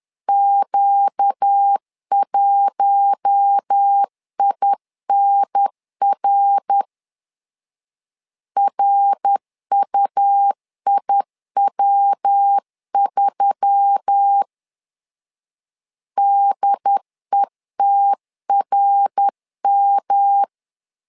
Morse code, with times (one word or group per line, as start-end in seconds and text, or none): 0.3-6.8 Q1INR
8.6-14.4 RUIW3
16.2-20.4 DETRM